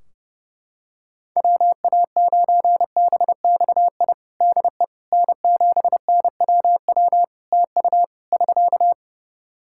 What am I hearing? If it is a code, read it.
Morse code, heard as a keyed tone, about 30 wpm, 700 Hz. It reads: WA9BXI DE N7NWW TU <SK>
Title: Practice QSO - Exchange 4 - Ending - 30wpm